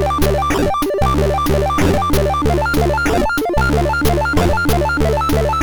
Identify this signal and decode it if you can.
There is no code.